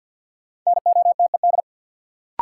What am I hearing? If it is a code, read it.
Morse code, heard as a keyed tone, about 53 wpm, 700 Hz. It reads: NOTED
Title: Sets of 5 Words - New Word Drill - 50wpm